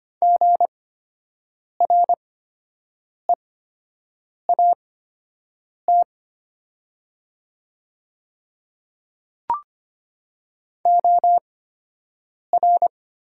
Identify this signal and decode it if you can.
Morse code, heard as a keyed tone, about 25 wpm, spaced as for 7 wpm, 700 Hz. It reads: GREAT OR
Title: 8x Character Spacing - Top 200 Words - 25wpm